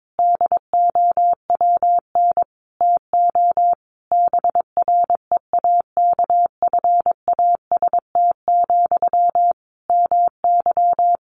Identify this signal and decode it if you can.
Morse code, heard as a keyed tone, about 22 wpm, 700 Hz. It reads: DOWN TO BREAKFAST, MY
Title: Sentences from Top 1500 Words 22wpm